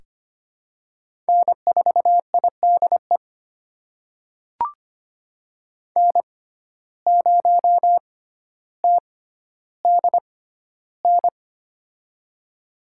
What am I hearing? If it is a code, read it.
Morse code, heard as a keyed tone, about 25 wpm, spaced as for 13 wpm, 700 Hz. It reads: N4IDE N0TDN
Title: USA Call Signs - 5 Characters - 6x Character Spacing - 25wpm